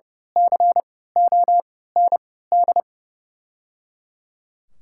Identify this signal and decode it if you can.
Morse code, heard as a keyed tone, about 30 wpm, spaced as for 17 wpm, 700 Hz. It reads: COND
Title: 3x Character Spacing - Top 500 Words - 30wpm